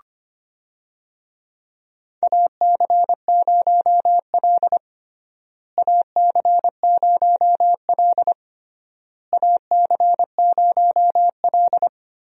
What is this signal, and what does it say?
Morse code, heard as a keyed tone, about 25 wpm, 700 Hz. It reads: AC0L AC0L AC0L
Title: USA Call Signs - 4 Characters - Repeated 3x - 25wpm